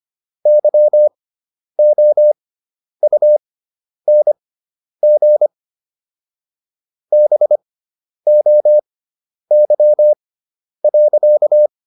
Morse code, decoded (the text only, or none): YOUNG BOY.